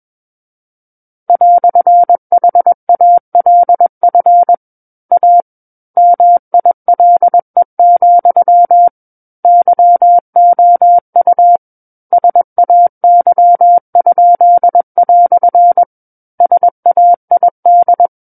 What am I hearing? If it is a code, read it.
Morse code, heard as a keyed tone, about 21 wpm, 700 Hz. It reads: "HALF A MILE, YOU SAY?" SAID
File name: Book One / 12-Chapter 12 - 21 WPM